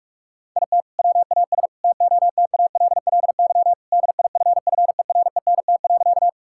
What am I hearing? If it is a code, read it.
Morse code, heard as a keyed tone, about 46 wpm, 700 Hz. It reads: IT WAS TOTALLY DIFFERENT.
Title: Sentences from Top 1600 Words 45wpm